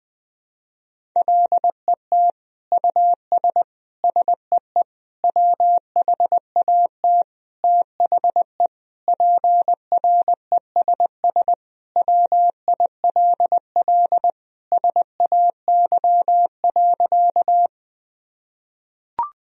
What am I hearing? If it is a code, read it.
Morse code, heard as a keyed tone, about 20 wpm, 700 Hz. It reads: LET US SEE WHAT THE PRESS WILL SAY.
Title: Sentences from Top 1000 Words - Review Part 1 - 20wpm